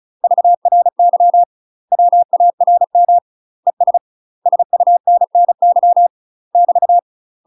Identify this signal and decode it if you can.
Morse code, heard as a keyed tone, about 34 wpm, 700 Hz. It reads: VRY WARM ES SUNNY <BT>
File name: UK MASTER QSOs TRAINING / 35WPM / UK MASTER QSOs 0013